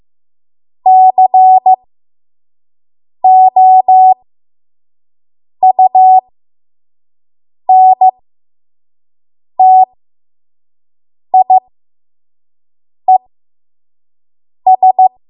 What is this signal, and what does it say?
Morse code, heard as a keyed tone, about 15 wpm, spaced as for 5 wpm, 750 Hz. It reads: COUNTIES
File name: Morse / 05WPM